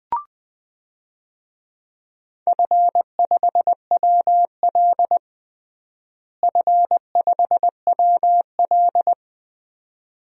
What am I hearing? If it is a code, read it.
Morse code, heard as a keyed tone, about 20 wpm, 700 Hz. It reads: F5WL F5WL
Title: DXCC-100 Call Signs - 4 Characters - Repeated 2x - 20wpm